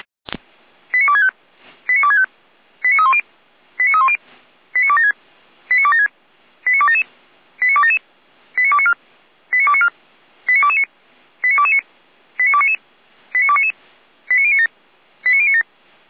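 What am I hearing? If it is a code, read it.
Five-tone selective call (ZVEI), heard as quick runs of five tones, tones of about 70 ms. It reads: 89265 89265 89310 89310 89276 89276 89277 89277 89294 89294 89229 89229 89299 89299 89097 89097
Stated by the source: ZVEI five-tone call received by SDR.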